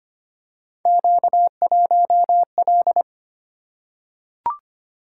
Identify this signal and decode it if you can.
Morse code, heard as a keyed tone, about 25 wpm, 700 Hz. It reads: Q1L
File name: Three Letter-Number Combinations - 25wpm